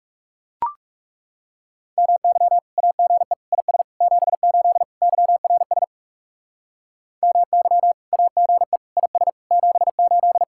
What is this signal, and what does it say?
Morse code, heard as a keyed tone, about 37 wpm, 700 Hz. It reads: MY AGE IS 78 YRS MY AGE IS 78
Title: Speed-Racing Common QSO Phrases - 30wpm